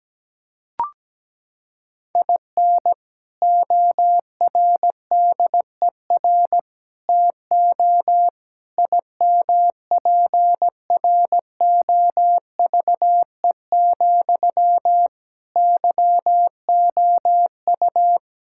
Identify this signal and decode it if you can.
Morse code, heard as a keyed tone, about 17 wpm, 700 Hz. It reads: IN ORDER TO IMPROVE, YOU
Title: Sentences from Top 1000 Words - Review Part 5 - 17wpm